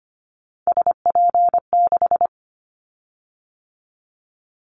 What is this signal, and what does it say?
Morse code, heard as a keyed tone, about 25 wpm, 700 Hz. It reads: SP6
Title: Callsign Prefixes for Top 25 DXCC - 25wpm